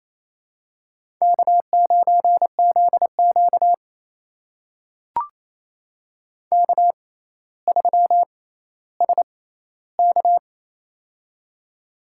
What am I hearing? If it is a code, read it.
Morse code, heard as a keyed tone, about 29 wpm, spaced as for 15 wpm, 700 Hz. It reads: K9ZQ K3SK
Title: USA Call Signs - 4 Characters - 6x Character Spacing - 28wpm